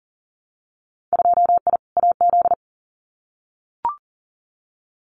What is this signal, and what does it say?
Morse code, heard as a keyed tone, about 41 wpm, 700 Hz. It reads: 2I AZ
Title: Winter Field Day Exchanges Encore 40wpm